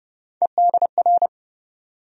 Morse code, heard as a keyed tone, about 30 wpm, 700 Hz. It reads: EDR
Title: USA Call Sign Postfixes - 3 Letters 30wpm